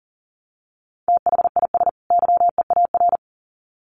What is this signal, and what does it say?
Morse code, heard as a keyed tone, about 40 wpm, 700 Hz. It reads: THIS YEAR
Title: Sets of 2 Words 40wpm